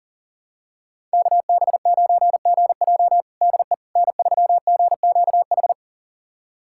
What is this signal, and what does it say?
Morse code, heard as a keyed tone, about 37 wpm, 700 Hz. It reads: KB9GJ DE N3GQH
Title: USA Callsign DE Callsign - 5 Characters - 40wpm